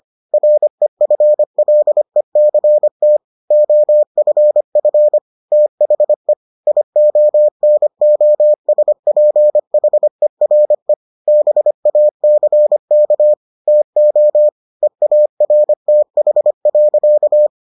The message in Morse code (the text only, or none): REFLECT OFF THE IONOSPHERE BACK TO EARTH.